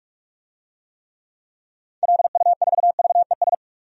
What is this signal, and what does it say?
Morse code, heard as a keyed tone, about 46 wpm, 700 Hz. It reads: LU4VES
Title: DXCC-100 Call Signs - 6 Characters - 45wpm